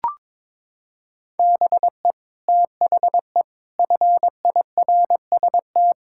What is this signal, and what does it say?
Morse code, heard as a keyed tone, about 22 wpm, 700 Hz. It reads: BE THE FIRST
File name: practice / 22wpm / Sets of 6 Words - 22wpm